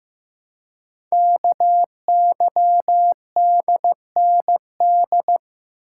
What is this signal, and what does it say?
Morse code, heard as a keyed tone, about 15 wpm, 700 Hz. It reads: KYDND
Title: Code Groups - 5 Letters - 15wpm